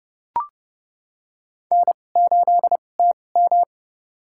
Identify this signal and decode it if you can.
Morse code, heard as a keyed tone, about 30 wpm, spaced as for 22 wpm, 700 Hz. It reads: N8TM
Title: USA Call Signs - 2x Character Spacing - 30wpm